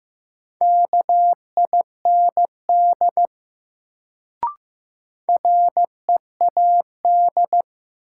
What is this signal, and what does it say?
Morse code, heard as a keyed tone, about 15 wpm, 700 Hz. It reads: KIND READ
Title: Sentences from Top 200 Words - New Word Drill - 15wpm